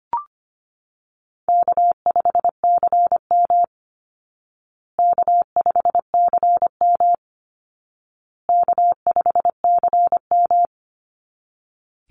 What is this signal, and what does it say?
Morse code, heard as a keyed tone, about 25 wpm, 700 Hz. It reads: K5CM K5CM K5CM
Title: Easy Call Signs Repeated 3x - 25wpm